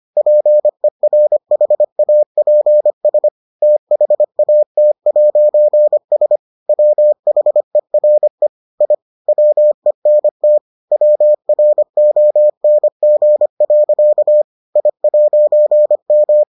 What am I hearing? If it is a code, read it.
Morse code, heard as a keyed tone, about 25 wpm, 600 Hz. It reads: PERHAPS THAT'S WHERE I WENT WRONG. I'M